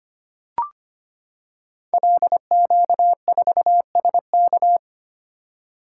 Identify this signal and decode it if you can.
Morse code, heard as a keyed tone, about 25 wpm, 700 Hz. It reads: LQ4SK
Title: Code Groups - 5 Letter:Numbers - 25wpm